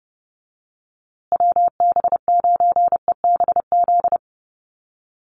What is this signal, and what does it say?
Morse code, heard as a keyed tone, about 30 wpm, 700 Hz. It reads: WB9EBZ
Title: USA Call Signs Repeated 3x 30wpm